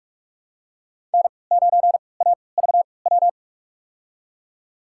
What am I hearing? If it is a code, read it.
Morse code, heard as a keyed tone, about 46 wpm, spaced as for 26 wpm, 700 Hz. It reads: N9AVW
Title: USA Call Signs - 5 Characters - 3x Character Spacing - 45wpm